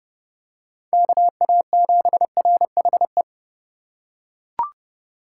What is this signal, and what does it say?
Morse code, heard as a keyed tone, about 29 wpm, 700 Hz. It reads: KA7RHE